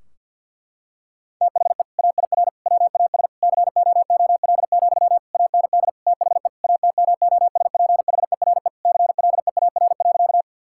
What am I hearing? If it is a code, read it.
Morse code, heard as a keyed tone, about 50 wpm, 700 Hz. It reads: THE AIR WAS COOL, AND THE ATMOSPHERE CLEAR.
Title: Sentences from Top 1600 Words 50wpm